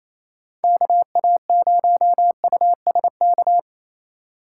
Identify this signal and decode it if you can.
Morse code, heard as a keyed tone, about 28 wpm, 700 Hz. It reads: KA0USK
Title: Speed Racing - USA 2x3 Call Signs 28wpm